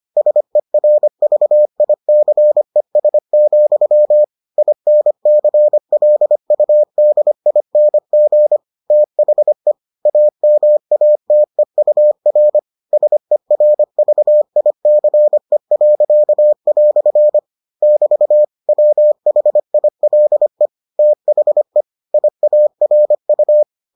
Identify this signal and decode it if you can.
Morse code, heard as a keyed tone, about 25 wpm, 600 Hz. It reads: SERVICES, INCLUDING THE AMATEUR SERVICE." <BT> WHILE THE IARU